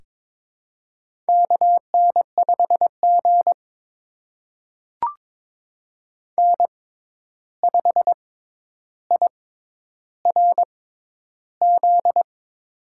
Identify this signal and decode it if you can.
Morse code, heard as a keyed tone, about 22 wpm, spaced as for 11 wpm, 700 Hz. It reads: KN5G N5IRZ
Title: USA Call Signs - 6x Character Spacing - 22wpm